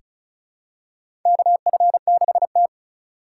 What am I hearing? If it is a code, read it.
Morse code, heard as a keyed tone, about 36 wpm, 700 Hz. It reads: KF6T
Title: K1USN SST QSO Practice - 35wpm